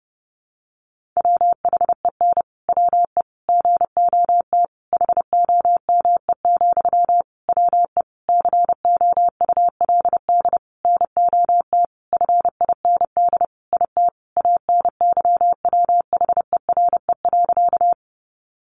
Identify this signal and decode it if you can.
Morse code, heard as a keyed tone, about 30 wpm, 700 Hz. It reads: WHEN WE GOT HOME, WE COULD NOT FIND IT ANYWHERE.
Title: Sentences from Top 1400 Words 30wpm